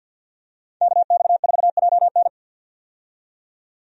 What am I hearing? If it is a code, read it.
Morse code, heard as a keyed tone, about 53 wpm, 700 Hz. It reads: KX4JN
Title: USA Call Signs - 5 Characters - 50wpm